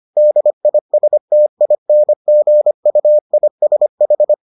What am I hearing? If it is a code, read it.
Morse code, heard as a keyed tone, about 25 wpm, 600 Hz. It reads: DISTINGUISH